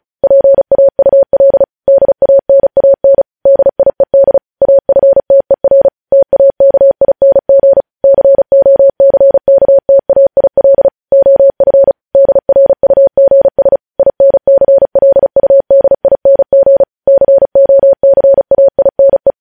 Morse code, heard as a keyed tone, about 36 wpm, 550 Hz. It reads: PAUL DANAN DIED AFTER TAKING COCKTAIL OF DRUGS INCLUDING COCAINE
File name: podcast / News headlines-1x-35x35